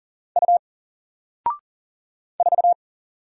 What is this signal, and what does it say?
Morse code, heard as a keyed tone, about 40 wpm, 700 Hz. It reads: U 4
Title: Single Letter-Number Alt 40wpm